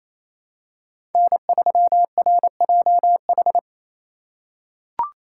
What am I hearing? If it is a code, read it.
Morse code, heard as a keyed tone, about 28 wpm, 700 Hz. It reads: N3RJH
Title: USA Call Signs - 5 Characters - 2x Character Spacing - 28wpm